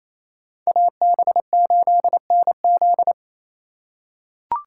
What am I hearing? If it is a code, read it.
Morse code, heard as a keyed tone, about 28 wpm, 700 Hz. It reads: AB8NZ
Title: USA Call Signs - 5 Characters - 4x Character Spacing - 28wpm